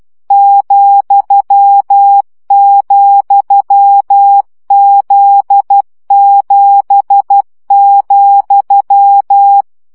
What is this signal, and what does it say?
Morse code, heard as a keyed tone, about 12 wpm, 800 Hz. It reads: ,,Z7,
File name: CW gyakorlas z 7 vesszo 10 baud